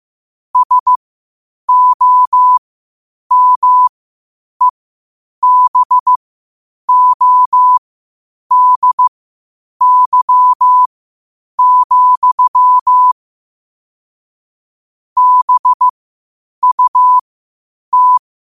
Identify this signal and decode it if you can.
Morse code, heard as a keyed tone, about 15 wpm, spaced as for 8 wpm, 1000 Hz. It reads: SOMEBODY, BUT
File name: qotdfiles / 8wpm